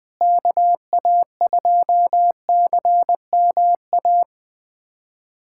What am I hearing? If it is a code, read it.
Morse code, heard as a keyed tone, about 20 wpm, 700 Hz. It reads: KA2CMA
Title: USA Callsign DE Callsign - 6 Characters - 20wpm